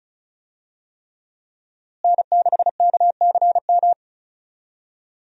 Morse code, heard as a keyed tone, about 36 wpm, 700 Hz. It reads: N6KCM
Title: USA Call Signs - 5 Characters - Repeated 3x - 35wpm